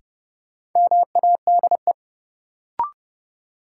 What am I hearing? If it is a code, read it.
Morse code, heard as a keyed tone, about 30 wpm, 700 Hz. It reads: MADE
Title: Sentences from Top 200 Words - New Word Drill - 30wpm